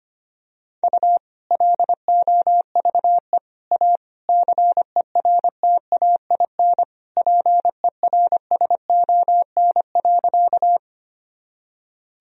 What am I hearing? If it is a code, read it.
Morse code, heard as a keyed tone, about 25 wpm, 700 Hz. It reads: U LOVE A CERTAIN PERSON.
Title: Sentences from Top 500 Words and Abbreviations 25wpm